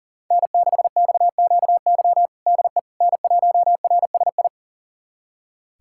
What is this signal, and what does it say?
Morse code, heard as a keyed tone, about 40 wpm, 700 Hz. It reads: N6XQY DE N1RSI
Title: USA Callsign DE Callsign - 5 Characters - 40wpm